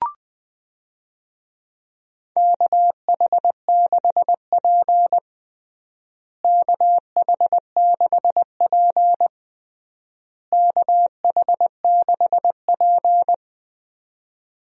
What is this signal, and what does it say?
Morse code, heard as a keyed tone, about 21 wpm, 690 Hz. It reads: KH6P KH6P KH6P
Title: DXCC-100 Call Signs - 4 Characters - Repeated 3x - 20wpm